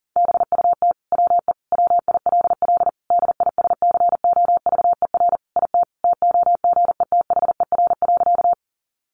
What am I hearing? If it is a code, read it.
Morse code, heard as a keyed tone, about 40 wpm, 700 Hz. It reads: BUT WE WILL DISCOVER IT TOGETHER.
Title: Sentences from Top 1100 Words 40wpm